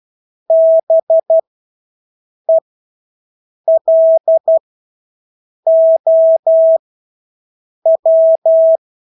Morse code, heard as a keyed tone, about 12 wpm, spaced as for 6 wpm, 650 Hz. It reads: BELOW
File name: Engelske ord 12-6 wpm